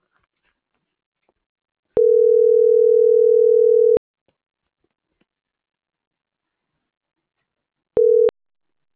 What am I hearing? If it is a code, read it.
Ringback tone (North America), 440 + 480 Hz.